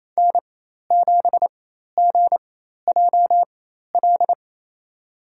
Morse code, heard as a keyed tone, about 28 wpm, spaced as for 13 wpm, 700 Hz. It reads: N7GJL
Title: USA Call Signs - 5 Characters - 4x Character Spacing - 28wpm